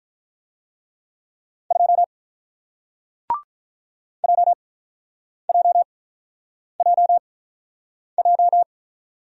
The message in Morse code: 2 JJJJ

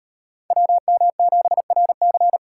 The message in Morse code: WM7RC

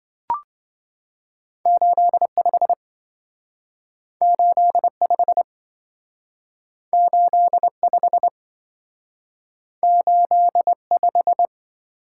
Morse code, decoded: 85 85 85 85

